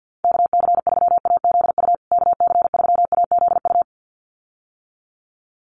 Morse code, heard as a keyed tone, about 50 wpm, 700 Hz. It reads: KC3AZU KC3AZU
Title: USA Call Signs Repeated 2x 50wpm